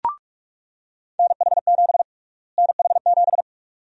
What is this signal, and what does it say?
Morse code, heard as a keyed tone, about 46 wpm, 700 Hz. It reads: NH7 NH7
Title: USA Call Sign Prefixes - Repeated 2x - 45wpm